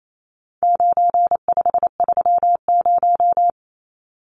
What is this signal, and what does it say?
Morse code, heard as a keyed tone, about 28 wpm, 700 Hz. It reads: 9530
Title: Numbers - 4-Digit 28wpm